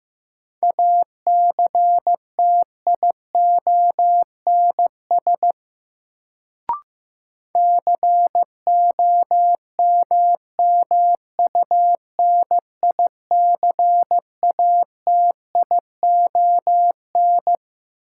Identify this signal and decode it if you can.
Morse code, heard as a keyed tone, about 15 wpm, 700 Hz. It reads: ACTIONS COMMUNICATION
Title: Sentences from Top 1000 Words - Review Part 6 - New Word Drill - 15wpm